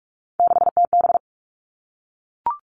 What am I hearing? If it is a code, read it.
Morse code, heard as a keyed tone, about 44 wpm, 700 Hz. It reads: 6TB
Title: All Cut Numbers - 3-Digit 45wpm